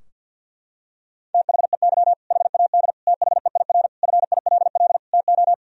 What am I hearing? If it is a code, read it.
Morse code, heard as a keyed tone, about 50 wpm, 700 Hz. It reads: THEY HAD THEIR FILL TO
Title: Sentences from Top 1000 Words - Review Part 3 - 50wpm